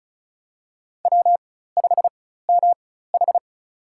Morse code, heard as a keyed tone, about 36 wpm, spaced as for 16 wpm, 700 Hz. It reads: W5MH
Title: USA Call Signs - 4 Characters - 4x Character Spacing - 35wpm